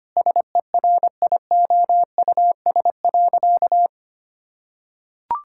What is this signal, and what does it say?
Morse code, heard as a keyed tone, about 25 wpm, 700 Hz. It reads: SERIOUS.
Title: Sentences from Top 1600 Words 25wpm